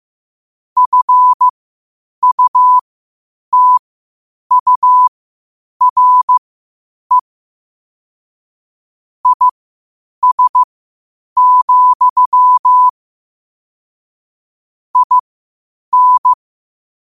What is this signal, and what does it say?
Morse code, heard as a keyed tone, about 15 wpm, spaced as for 8 wpm, 1000 Hz. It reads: FUTURE IS, IN
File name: qotdfiles / 8wpm